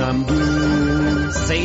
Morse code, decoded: none